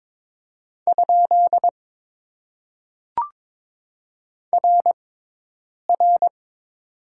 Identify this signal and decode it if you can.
Morse code, heard as a keyed tone, about 22 wpm, spaced as for 8 wpm, 700 Hz. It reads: ? RR
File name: Top 90 QSO Elements - 6x Character Spacing - 22wpm